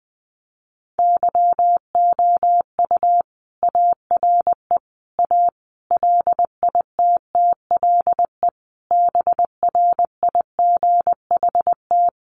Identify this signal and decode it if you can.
Morse code, heard as a keyed tone, about 20 wpm, 700 Hz. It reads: YOU ARE A LITTLE BRIGHT
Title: Sentences from Top 1000 Words 20wpm